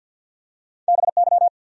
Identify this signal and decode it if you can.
Morse code, heard as a keyed tone, about 53 wpm, 700 Hz. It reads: BY
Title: Top 100 Words Repeated 50wpm